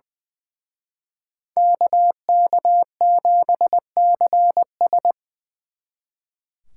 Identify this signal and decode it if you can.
Morse code, heard as a keyed tone, about 24 wpm, 700 Hz. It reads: KK7CS
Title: Speed Racing - USA Call Signs 20wpm